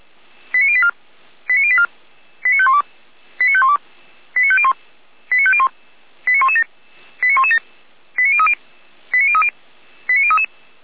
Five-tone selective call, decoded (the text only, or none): 89084 89084 89512 89512 89501 89501 89117 89117 89030 89030 89033